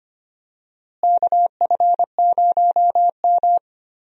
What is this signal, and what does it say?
Morse code, heard as a keyed tone, about 25 wpm, 700 Hz. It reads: KF0M
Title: USA Call Signs - 4 Characters - Repeated 2x - 25wpm